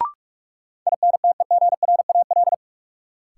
Morse code, heard as a keyed tone, about 44 wpm, 700 Hz. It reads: INTEGRAL
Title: Sets of 5 Words - New Word Drill - 45wpm